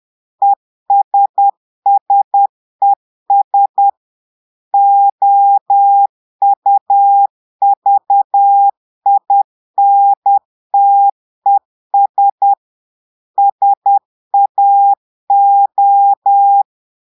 Morse code, heard as a keyed tone, about 10 wpm, 800 Hz. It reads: ESSES OUVINTES SAO
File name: portugais texte en clair 2 vitesse 10 mots minutes